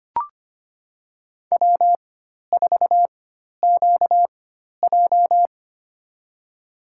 Morse code, heard as a keyed tone, about 25 wpm, spaced as for 12 wpm, 700 Hz. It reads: W4QJ
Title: USA Call Signs - 4 Characters - 4x Character Spacing - 25wpm